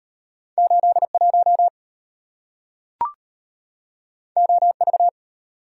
Morse code, heard as a keyed tone, about 34 wpm, 700 Hz. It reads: 81 OV